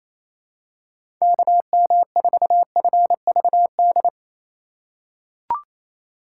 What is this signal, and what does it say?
Morse code, heard as a keyed tone, about 28 wpm, 700 Hz. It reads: KM4FVD
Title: USA Call Signs - 6 Characters - 4x Character Spacing - 28wpm